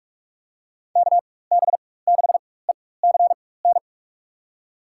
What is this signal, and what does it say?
Morse code, heard as a keyed tone, about 45 wpm, spaced as for 21 wpm, 700 Hz. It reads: KB6ECN
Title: USA Call Signs - 6 Characters - 4x Character Spacing - 45wpm